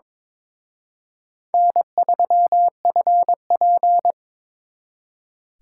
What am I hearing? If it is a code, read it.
Morse code, heard as a keyed tone, about 26 wpm, 700 Hz. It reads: N3FP
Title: Speed Racing - USA 1x2 Call Signs 22wpm